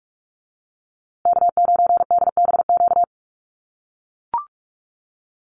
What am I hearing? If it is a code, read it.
Morse code, heard as a keyed tone, about 44 wpm, 700 Hz. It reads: K9DBQ
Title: USA Call Signs Repeated 3x 45wpm